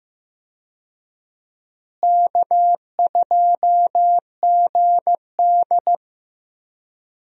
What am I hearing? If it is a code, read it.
Morse code, heard as a keyed tone, about 15 wpm, 700 Hz. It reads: K2GD
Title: USA Call Signs - 4 Characters - Repeated 2x - 15wpm